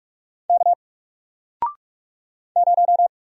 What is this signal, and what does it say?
Morse code, heard as a keyed tone, about 44 wpm, 700 Hz. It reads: K 0
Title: Single Letter-Number Alt 45wpm